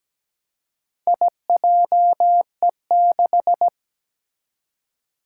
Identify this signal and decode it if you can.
Morse code, heard as a keyed tone, about 17 wpm, 700 Hz. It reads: IJE6
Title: Four Challenging Letter-Number Combinations 17wpm